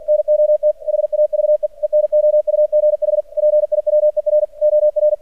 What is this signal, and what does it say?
Morse code, heard as a keyed tone, about 50 wpm, 600 Hz. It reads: NOT HAVE ENOUGH PIWER OR
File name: archived / CRSnet